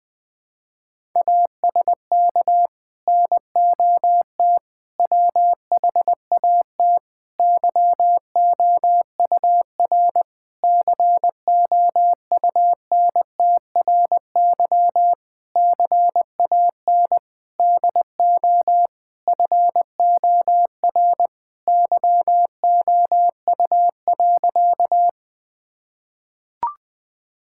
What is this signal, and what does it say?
Morse code, heard as a keyed tone, about 20 wpm, 700 Hz. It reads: ASK NOT WHAT YOUR COUNTRY CAN DO FOR YOU.
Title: Sentences with 8-Word Phrases - 20wpm